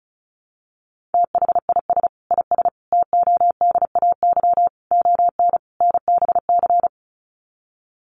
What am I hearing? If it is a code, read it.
Morse code, heard as a keyed tone, about 34 wpm, 700 Hz. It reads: THIS IS TODAY ON NBC
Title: Sets of 5 Words Encore 35wpm